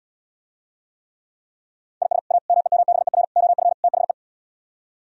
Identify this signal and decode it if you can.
Morse code, heard as a keyed tone, about 77 wpm, 700 Hz. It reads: IS A MEMBER OF THE